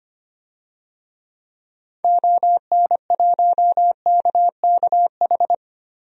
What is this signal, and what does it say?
Morse code, heard as a keyed tone, about 25 wpm, 700 Hz. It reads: ON1KKH
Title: DXCC-100 Call Signs - 6 Characters - Repeated 2x - 25wpm